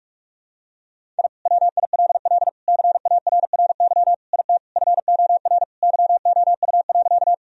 Morse code, heard as a keyed tone, about 42 wpm, 700 Hz. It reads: I WILL CARRY IT FOR YOU.
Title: Sentences from Top 1000 Words - Review Part 1 - 45wpm